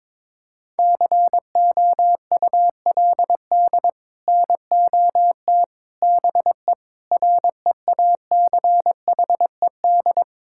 Morse code, heard as a keyed tone, about 22 wpm, 700 Hz. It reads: COULD NOT BE REACHED
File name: practice / 22wpm / Sets of 6 Words - 22wpm